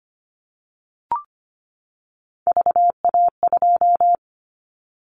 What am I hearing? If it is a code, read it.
Morse code, heard as a keyed tone, about 25 wpm, 700 Hz. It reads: VA2
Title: Callsign Prefixes for Top 50 DXCC - 25wpm